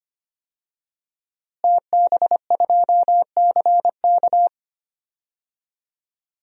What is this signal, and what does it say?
Morse code, heard as a keyed tone, about 25 wpm, 700 Hz. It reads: TB2CK
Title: DXCC-100 Call Signs - 5 Characters - Repeated 2x - 25wpm